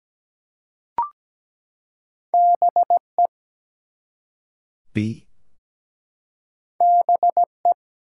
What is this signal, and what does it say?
Morse code, heard as a keyed tone, about 17 wpm, 700 Hz. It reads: BE BE
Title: Two Letter Words - 17wpm